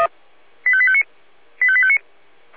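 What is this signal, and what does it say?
Five-tone selective call (ZVEI), heard as quick runs of five tones, tones of about 70 ms. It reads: 85859 85859
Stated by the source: Five-tone ZVEI selective call 85820, received by SDR.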